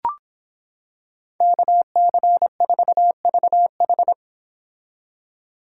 Morse code, heard as a keyed tone, about 26 wpm, 700 Hz. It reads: KC4VH